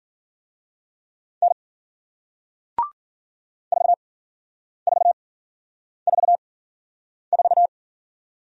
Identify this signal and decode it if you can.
Morse code, heard as a keyed tone, about 54 wpm, 700 Hz. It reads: N 4 4 4 4